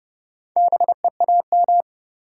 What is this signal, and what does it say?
Morse code, heard as a keyed tone, about 30 wpm, 700 Hz. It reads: BEAM
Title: Top 90 QSO Elements 30wpm